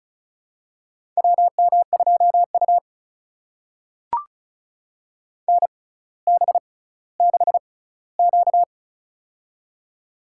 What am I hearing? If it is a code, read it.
Morse code, heard as a keyed tone, about 36 wpm, spaced as for 18 wpm, 700 Hz. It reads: WM2U NB6Q
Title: USA Call Signs - 4 Characters - 6x Character Spacing - 35wpm